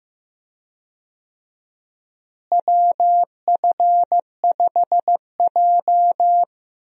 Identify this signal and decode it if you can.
Morse code, heard as a keyed tone, about 15 wpm, 700 Hz. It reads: WF5J
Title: USA Call Signs - 4 Characters - Repeated 2x - 15wpm